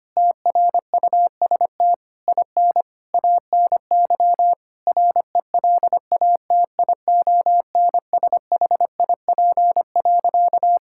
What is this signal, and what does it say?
Morse code, heard as a keyed tone, about 25 wpm, 700 Hz. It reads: TRUST IN ANY RELATIONSHIP.
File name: practice / 25wpm / Difficult Sentences with Hints - 25wpm